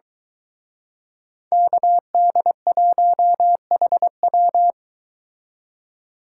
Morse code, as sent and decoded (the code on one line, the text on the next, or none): -.- -.. .---- .... .--
KD1HW